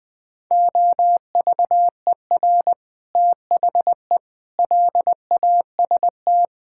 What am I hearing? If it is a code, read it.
Morse code, heard as a keyed tone, about 20 wpm, 700 Hz. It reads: OVER THE LAST
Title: Sets of 6 Words - 20wpm